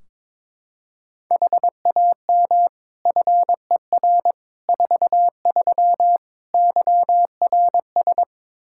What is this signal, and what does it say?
Morse code, heard as a keyed tone, about 22 wpm, 700 Hz. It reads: HAM FER 43 YRS
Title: K1USN SST QSO Practice 22wpm